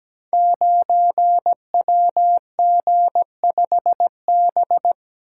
Morse code, heard as a keyed tone, about 17 wpm, 700 Hz. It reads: 9WG5B
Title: Code Groups - 5 Letter:Numbers - 17wpm